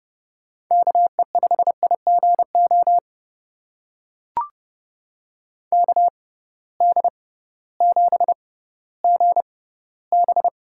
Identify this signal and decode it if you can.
Morse code, heard as a keyed tone, about 30 wpm, spaced as for 16 wpm, 700 Hz. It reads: KE5IGO KD7GB
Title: USA Call Signs - 6 Characters - 6x Character Spacing - 30wpm